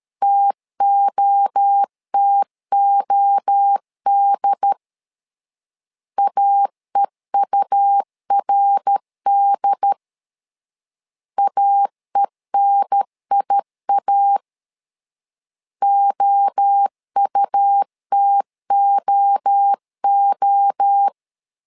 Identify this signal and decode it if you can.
Morse code, heard as a keyed tone, about 13 wpm, 800 Hz. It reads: TOTOD AEURD AENIA OUTOO
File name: LEZIONI / 3 MWRO / 3 MWRO 12wpm 1730